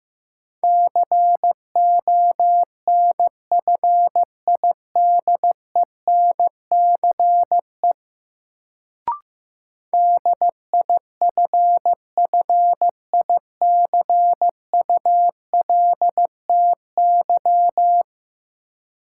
Morse code, heard as a keyed tone, about 15 wpm, 700 Hz. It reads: CONFIDENCE DIFFICULTY
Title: Sentences from Top 1500 Words - New Word Drill Alt - 15wpm